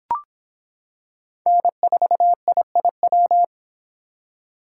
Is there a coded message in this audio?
Yes: Morse code from 1.5 to 3.4 s, heard as a keyed tone, about 25 wpm, 700 Hz. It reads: N4IIW